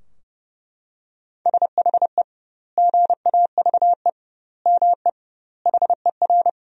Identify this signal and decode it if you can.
Morse code, heard as a keyed tone, about 30 wpm, 700 Hz. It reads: SHE GAVE ME HER
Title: 2x Word-Spacing - Sentences from Top 500 Words - 30wpm